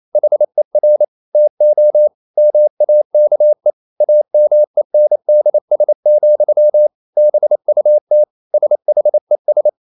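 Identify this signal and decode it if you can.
Morse code, heard as a keyed tone, about 28 wpm, 600 Hz. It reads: HER TO MAKE AMENDS, BUT SHES
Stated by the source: Morse code practice at 28 wpm, text of Ken Follett's novel Never in English, from CW med Gnister.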